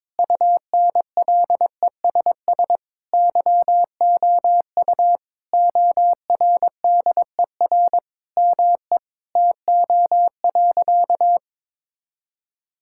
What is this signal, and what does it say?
Morse code, heard as a keyed tone, about 22 wpm, 700 Hz. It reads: UNLESS YOU ORDER ME TO.
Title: Sentences from Top 1000 Words - Review Part 6 - 22wpm